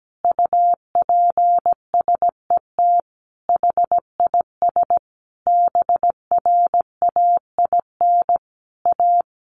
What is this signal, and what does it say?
Morse code, heard as a keyed tone, about 17 wpm, 700 Hz. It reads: UPSET HIS BRAIN A
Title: Sentences from Top 1500 Words 17wpm